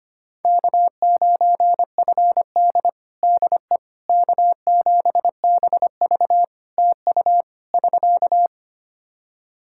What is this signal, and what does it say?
Morse code, heard as a keyed tone, about 25 wpm, 700 Hz. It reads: K9FD DE K7BV TU <SK>
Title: Practice QSO - Exchange 4 - Ending - 25wpm